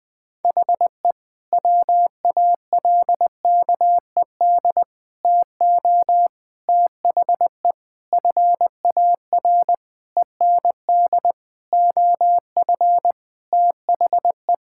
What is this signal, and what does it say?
Morse code, heard as a keyed tone, about 19 wpm, 700 Hz. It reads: HE WALKED TO THE FAR END OF THE
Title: Sentences with 5-Word Phrases - Spoken Beginning - 20wpm